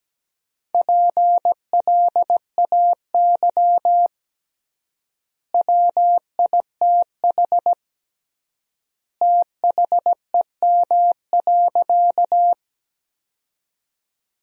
Morse code, heard as a keyed tone, about 17 wpm, 700 Hz. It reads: PLAY WITH THEM.
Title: 3x Word-Spacing - Sentences from Top 200 Words - 17wpm